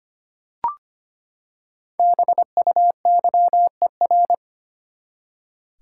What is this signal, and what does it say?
Morse code, heard as a keyed tone, about 25 wpm, 700 Hz. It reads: BUYER
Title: Five Letter Words - 25wpm